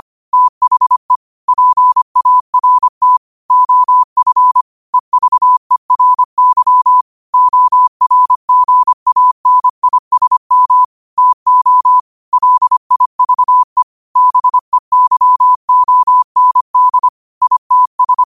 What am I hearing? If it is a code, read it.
Morse code, heard as a keyed tone, about 25 wpm, 1000 Hz. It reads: THE PART OF EVERY ORGANISM TO LIVE BEYOND ITS
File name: qotdfiles / 25wpm